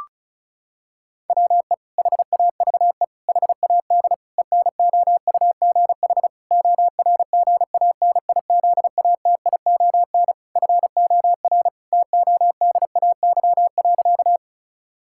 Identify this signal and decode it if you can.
Morse code, heard as a keyed tone, about 34 wpm, 700 Hz. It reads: WE HAVE HAD ENOUGH ORGANIZATION FOR TODAY.
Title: Sentences from Top 1000 Words - Review Part 5 - 35wpm